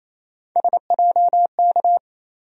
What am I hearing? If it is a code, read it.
Morse code, heard as a keyed tone, about 28 wpm, 700 Hz. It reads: SJK
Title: USA Call Sign Postfixes - 3 Letters 28wpm